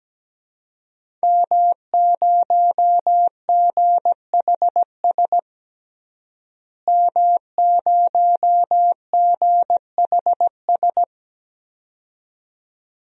Morse code, heard as a keyed tone, about 17 wpm, 700 Hz. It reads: M0GHS M0GHS
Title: DXCC-100 Call Signs - 5 Characters - Repeated 2x - 17wpm